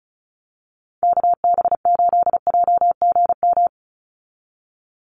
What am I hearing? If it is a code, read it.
Morse code, heard as a keyed tone, about 34 wpm, 700 Hz. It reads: KB8JGM
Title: USA Call Signs 35wpm